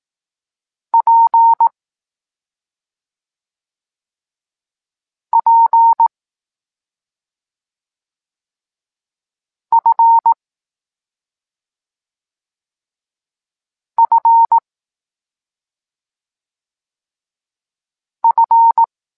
Morse code, heard as a keyed tone, about 18 wpm, 920 Hz. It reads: P P F F F
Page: Tekst 25 – Nauka liter P F